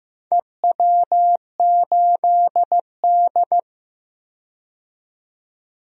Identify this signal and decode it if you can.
Morse code, heard as a keyed tone, about 15 wpm, 700 Hz. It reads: EW8D
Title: DXCC-100 Call Signs - 4 Characters - 15wpm